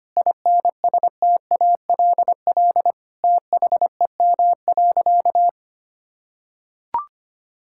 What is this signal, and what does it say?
Morse code, heard as a keyed tone, about 25 wpm, 700 Hz. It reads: INSTALL THEM.
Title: Sentences from Top 1600 Words 25wpm